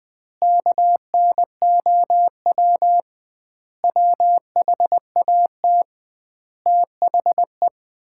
Morse code, heard as a keyed tone, about 20 wpm, 700 Hz. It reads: KNOW WHAT THE
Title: 2x Word-Spacing - Sets of 5 Words - 20wpm